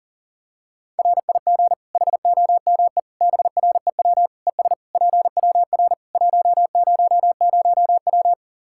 Morse code, heard as a keyed tone, about 40 wpm, 700 Hz. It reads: RIG HOME BREW ES PWR 100W
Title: Practice QSO - Exchange 2 - Info - 40wpm